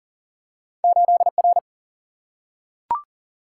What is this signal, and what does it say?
Morse code, heard as a keyed tone, about 39 wpm, 700 Hz. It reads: 8R
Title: Single-Letter Single-Number 40wpm